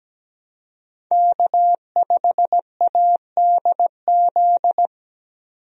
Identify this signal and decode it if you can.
Morse code, heard as a keyed tone, about 17 wpm, 700 Hz. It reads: K5ADZ
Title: USA Call Signs - 5 Characters - 17wpm